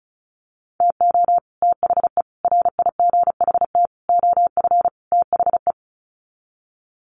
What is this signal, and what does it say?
Morse code, heard as a keyed tone, about 34 wpm, 700 Hz. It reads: TO THE RIGHT OF THE
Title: Sets of 5 Words Encore 35wpm